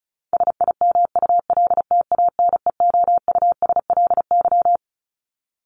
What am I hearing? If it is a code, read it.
Morse code, heard as a keyed tone, about 34 wpm, 700 Hz. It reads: SIMULTANEOUSLY
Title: LY Suffix - Top 100 Words with Base Words 35wpm